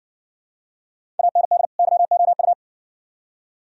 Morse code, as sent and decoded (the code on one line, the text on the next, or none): .- -. -.. / -.-- --- ..-
AND YOU